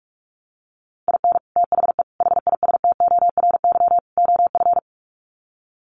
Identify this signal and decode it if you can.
Morse code, heard as a keyed tone, about 44 wpm, 700 Hz. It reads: IN THE HISTORY OF
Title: Sets of 4 Words 45wpm